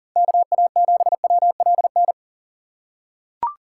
Morse code, heard as a keyed tone, about 37 wpm, 700 Hz. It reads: KA7WLN
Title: USA Call Signs - 3x Character Spacing - 40wpm